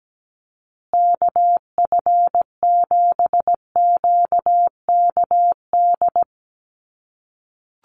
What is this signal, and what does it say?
Morse code, heard as a keyed tone, about 17 wpm, 700 Hz. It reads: KF7QKD
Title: USA Call Signs Repeated 2x 17wpm